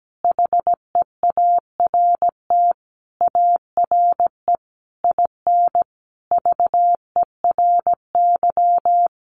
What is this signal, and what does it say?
Morse code, heard as a keyed tone, about 17 wpm, 700 Hz. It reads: HEART ARE IN VERY
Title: Sentences from Top 1300 Words 17wpm